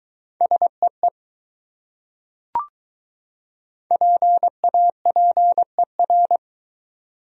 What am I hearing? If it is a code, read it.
Morse code, heard as a keyed tone, about 22 wpm, 700 Hz. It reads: SEE PAPER